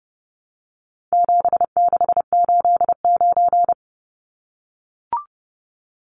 Morse code, heard as a keyed tone, about 30 wpm, 700 Hz. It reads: 7689
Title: Numbers - 4-Digit 30wpm